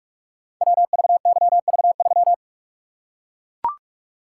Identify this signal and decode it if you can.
Morse code, heard as a keyed tone, about 44 wpm, 700 Hz. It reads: WVYV3
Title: Code Groups - 5 Letter:Numbers - 45wpm